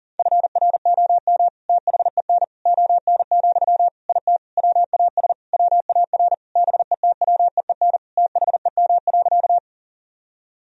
Morse code, heard as a keyed tone, about 40 wpm, 700 Hz. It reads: FROM THEN ON, IT WAS WAR BETWEEN THEM.
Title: Sentences from Top 1000 Words - Review Part 2 - 40wpm